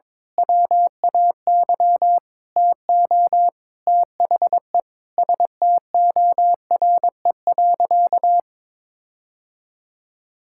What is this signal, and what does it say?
Morse code, heard as a keyed tone, about 22 wpm, 700 Hz. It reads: WAY TO THE STORE.
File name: practice / 22wpm / Sentences with 8-Word Phrases - 22wpm